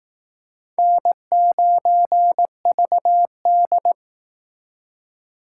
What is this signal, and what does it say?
Morse code, heard as a keyed tone, about 18 wpm, 700 Hz. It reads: N9VD
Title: Speed Racing - USA Call Signs 15wpm